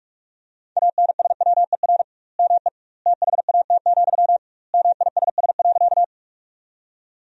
Morse code, heard as a keyed tone, about 46 wpm, 700 Hz. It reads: ANSWER ME THAT, MISS.
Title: 2x Word-Spacing - Sentences from Top 500 Words - 45wpm